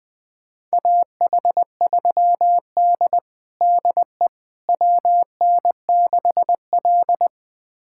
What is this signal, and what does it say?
Morse code, heard as a keyed tone, about 21 wpm, 700 Hz. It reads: AH3D DE WN6L
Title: USA Callsign DE Callsign - 4 Characters - 20wpm